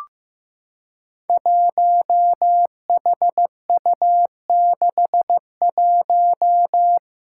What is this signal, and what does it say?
Morse code, heard as a keyed tone, about 15 wpm, 700 Hz. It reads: 1HU61
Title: Five Challenging Letter-Number Combinations 15wpm